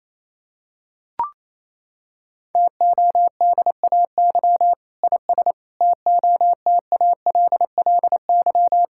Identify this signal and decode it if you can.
Morse code, heard as a keyed tone, about 27 wpm, 700 Hz. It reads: TODAY IS TOTALLY
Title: Sentences from Top 1600 Words 28wpm